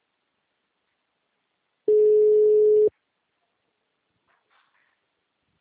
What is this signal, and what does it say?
Ringback tone (Europe, 425 Hz).